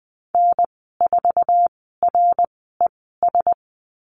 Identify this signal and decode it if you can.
Morse code, heard as a keyed tone, about 20 wpm, spaced as for 15 wpm, 700 Hz. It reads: N4RES
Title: 062 Lesson 11g - Letter h - Call Signs 20wpm